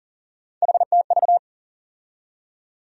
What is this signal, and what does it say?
Morse code, heard as a keyed tone, about 40 wpm, 700 Hz. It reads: HTV
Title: USA Call Sign Postfixes - 3 Letters 40wpm